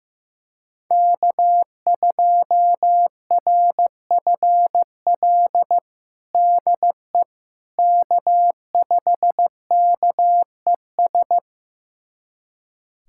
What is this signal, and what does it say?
Morse code, heard as a keyed tone, about 15 wpm, 700 Hz. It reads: K2RFL DE K5KES
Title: USA Callsign DE Callsign - 5 Characters - 15wpm